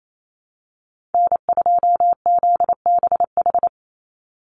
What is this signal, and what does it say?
Morse code, heard as a keyed tone, about 28 wpm, 700 Hz. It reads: N2ZBH
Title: USA Call Signs 28wpm